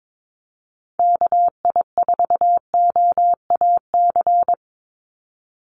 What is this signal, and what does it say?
Morse code, heard as a keyed tone, about 22 wpm, 700 Hz. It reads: KI4OAC
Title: USA Call Signs 22wpm